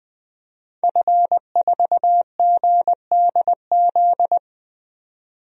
Morse code, heard as a keyed tone, about 20 wpm, 700 Hz. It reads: F4GDZ
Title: DXCC-100 Call Signs - 5 Characters - 20wpm